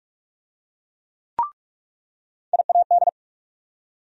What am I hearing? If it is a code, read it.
Morse code, heard as a keyed tone, about 46 wpm, 700 Hz. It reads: IAD
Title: USA Call Sign Postfixes - 3 Letters 45wpm